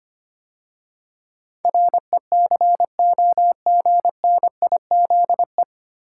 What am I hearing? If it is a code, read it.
Morse code, heard as a keyed tone, about 25 wpm, 700 Hz. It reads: RECOGNIZE
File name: practice / 25wpm / 9 Letter Words - First 3 Letters Given - 25wpm